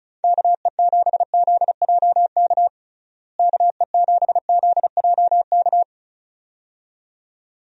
Morse code, heard as a keyed tone, about 34 wpm, 700 Hz. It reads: KE7ZJK KE7ZJK
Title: USA Call Signs - 6 Characters - Repeated 2x - 35wpm